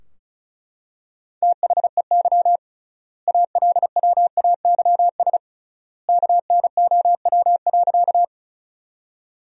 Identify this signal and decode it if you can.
Morse code, heard as a keyed tone, about 36 wpm, 700 Hz. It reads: THEY ALWAYS KNOW.
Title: 3x Word-Spacing - Sentences from Top 300 Words - 35wpm